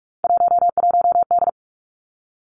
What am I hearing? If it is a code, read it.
Morse code, heard as a keyed tone, about 46 wpm, 700 Hz. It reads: 11D
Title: Three Challenging Letter-Number Combinations 45wpm